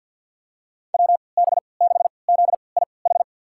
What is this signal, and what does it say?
Morse code, heard as a keyed tone, about 53 wpm, spaced as for 29 wpm, 700 Hz. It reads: WB6ZIH
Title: USA Call Signs - 6 Characters - 3x Character Spacing - 50wpm